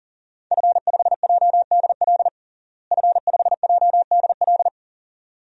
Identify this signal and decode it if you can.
Morse code, heard as a keyed tone, about 40 wpm, 700 Hz. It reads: F5JDL F5JDL
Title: DXCC-100 Call Signs - 5 Characters - Repeated 2x - 40wpm